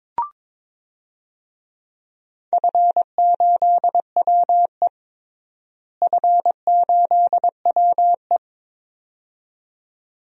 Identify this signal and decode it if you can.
Morse code, heard as a keyed tone, about 22 wpm, 700 Hz. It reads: F8WE F8WE
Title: DXCC-100 Call Signs - 4 Characters - Repeated 2x - 22wpm